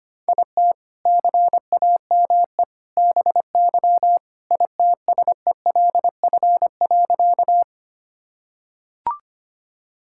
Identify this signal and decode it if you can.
Morse code, heard as a keyed tone, about 25 wpm, 700 Hz. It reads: IT CAME BY ITSELF.
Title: Sentences from Top 1000 Words - Review Part 5 - 25wpm